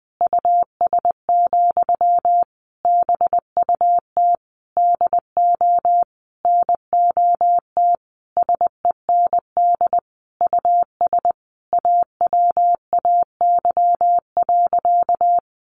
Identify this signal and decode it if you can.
Morse code, heard as a keyed tone, about 20 wpm, 700 Hz. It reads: US, BUT DO NOT SEND US AWAY.
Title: Sentences from Top 1500 Words 20wpm